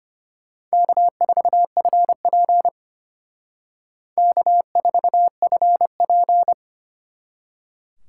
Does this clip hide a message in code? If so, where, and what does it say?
Yes: Morse code from 0.7 to 6.5 s, heard as a keyed tone, about 32 wpm, 700 Hz. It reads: K4FP K4FP